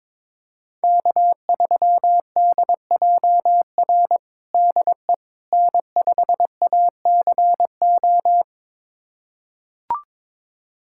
Morse code, heard as a keyed tone, about 22 wpm, 700 Hz. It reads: K3DJR DE N5ACO
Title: USA Callsign DE Callsign - 5 Characters - 22wpm